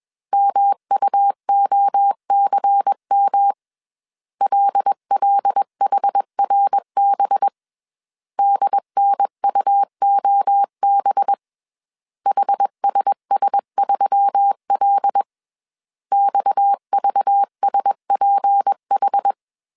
Morse code, heard as a keyed tone, about 21 wpm, 800 Hz. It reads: MUOCM LLHRB DNUOB HSS3L XVSPH
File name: CW / DET / 1 6 1500C S B 20wpm 2048